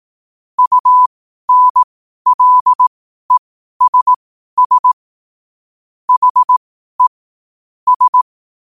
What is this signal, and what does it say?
Morse code, heard as a keyed tone, about 18 wpm, spaced as for 12 wpm, 1000 Hz. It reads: UNLESS HE S
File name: qotdfiles / 12wpm